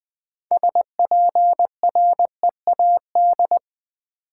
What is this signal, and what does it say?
Morse code, heard as a keyed tone, about 21 wpm, 700 Hz. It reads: SPREAD
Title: Six Letter Words - 20wpm